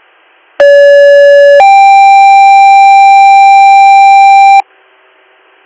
Two-tone page, heard as a one-second tone, then a three-second tone, about 570 then 790 Hz, with noise. It is heard from 0.6 to 4.6 s.